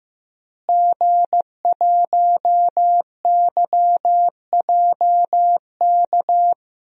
Morse code, heard as a keyed tone, about 16 wpm, 700 Hz. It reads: G1YJK
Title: DXCC-100 Call Signs - 5 Characters - Repeated 2x - 15wpm